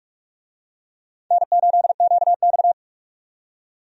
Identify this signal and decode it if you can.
Morse code, heard as a keyed tone, about 46 wpm, 700 Hz. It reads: N8QX
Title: USA Call Signs - 4 Characters - 45wpm